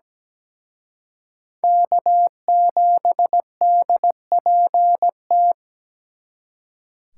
Morse code, heard as a keyed tone, about 20 wpm, 700 Hz. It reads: K7DPT
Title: Speed Racing - USA 1x3 Call Signs 17wpm